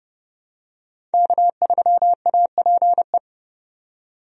Morse code, heard as a keyed tone, about 30 wpm, 700 Hz. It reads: K3APE
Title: USA Call Signs - 5 Characters - 30wpm